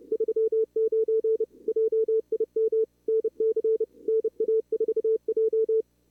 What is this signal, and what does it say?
Morse code, heard as a keyed tone, about 30 wpm, 440 Hz. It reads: 39 JIM NC NA4J
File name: CWSprint / Feb2022 / k5tr-feb-na-sprint-cw-k5ot -2300